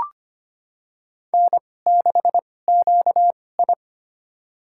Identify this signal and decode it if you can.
Morse code, heard as a keyed tone, about 25 wpm, spaced as for 18 wpm, 700 Hz. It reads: N6QI
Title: USA Call Signs - 4 Characters - 2x Character Spacing - 25wpm